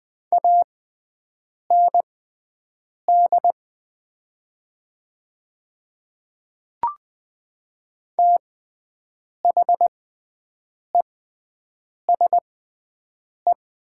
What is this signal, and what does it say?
Morse code, heard as a keyed tone, about 20 wpm, spaced as for 7 wpm, 700 Hz. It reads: AND THESE
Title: 6x Character Spacing - Top 100 Words - 20wpm